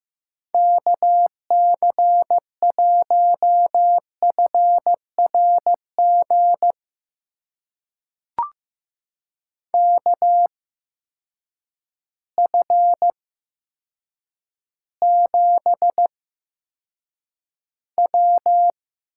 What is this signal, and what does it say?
Morse code, heard as a keyed tone, about 16 wpm, spaced as for 8 wpm, 700 Hz. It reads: KC1FRG KF7W
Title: USA Call Signs - 6 Characters - 8x Character Spacing - 15wpm